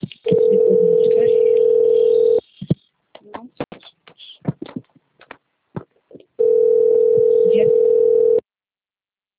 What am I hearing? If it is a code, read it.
Ringback tone (North America), 440 + 480 Hz.
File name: rg-1600-unknown- - - .81451